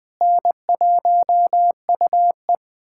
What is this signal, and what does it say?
Morse code, heard as a keyed tone, about 20 wpm, 700 Hz. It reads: N1UE
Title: USA Callsign DE Callsign - 4 Characters - 20wpm